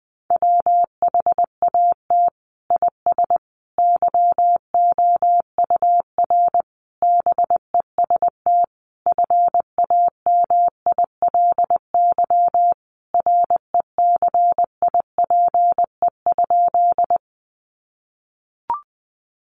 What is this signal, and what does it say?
Morse code, heard as a keyed tone, about 20 wpm, 700 Hz. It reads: WHAT IS YOUR BEST FAMILY RECIPE?
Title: Sentences from Top 1200 Words 20wpm